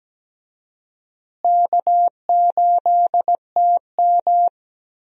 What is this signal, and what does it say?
Morse code, heard as a keyed tone, about 17 wpm, 700 Hz. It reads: K8TM
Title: USA Call Signs - 4 Characters - Repeated 2x - 17wpm